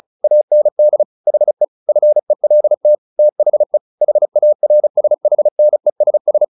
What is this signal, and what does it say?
Morse code, heard as a keyed tone, about 34 wpm, 600 Hz. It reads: AND HE FELT THE HARSHNESS
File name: the old man and the sea / the old man and the sea 1 35wpm